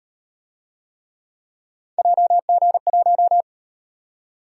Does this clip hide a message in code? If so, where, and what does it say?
Yes: Morse code from 2.0 to 3.4 s, heard as a keyed tone, about 36 wpm, 700 Hz. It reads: JG1